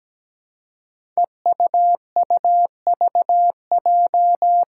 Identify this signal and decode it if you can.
Morse code, heard as a keyed tone, about 17 wpm, 700 Hz. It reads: EUUVJ
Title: Five Challenging Letter-Number Combinations 17wpm